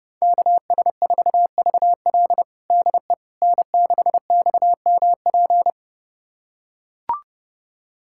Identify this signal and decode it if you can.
Morse code, heard as a keyed tone, about 30 wpm, 700 Hz. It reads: KS4VL DE N6XMP
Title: USA Callsign DE Callsign - 5 Characters - 30wpm